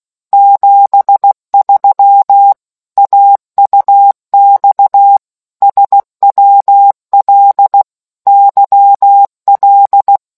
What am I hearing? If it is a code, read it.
Morse code, heard as a keyed tone, about 16 wpm, 800 Hz. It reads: 73 AUX SWL YL